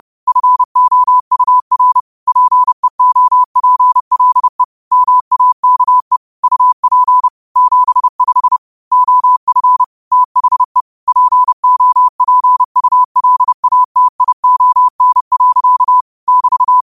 Morse code, heard as a keyed tone, about 30 wpm, 1000 Hz. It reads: FOUR PEOPLE MAKE UP 75 OF THE POPULATION. <BT>